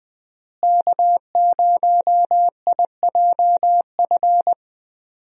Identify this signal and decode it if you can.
Morse code, heard as a keyed tone, about 21 wpm, 700 Hz. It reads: K0IJF
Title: USA Call Signs - 5 Characters - Repeated 3x - 20wpm